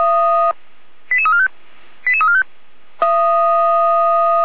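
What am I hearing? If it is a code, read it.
Five-tone selective call (ZVEI), heard as quick runs of five tones, tones of about 70 ms. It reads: 88345 88345
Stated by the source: ZVEI five-tone call 88129 received on SDR, fire brigade Feuerwehr Orlamünde.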